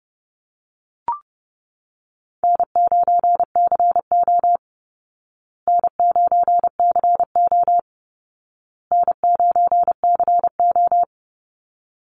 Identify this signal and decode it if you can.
Morse code, heard as a keyed tone, about 30 wpm, 700 Hz. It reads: N9CO N9CO N9CO
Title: Easy Call Signs Repeated 3x - 30wpm